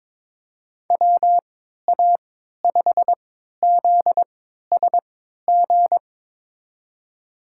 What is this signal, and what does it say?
Morse code, heard as a keyed tone, about 22 wpm, spaced as for 13 wpm, 700 Hz. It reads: WA5ZSG
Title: USA Call Signs - 6 Characters - 3x Character Spacing - 22wpm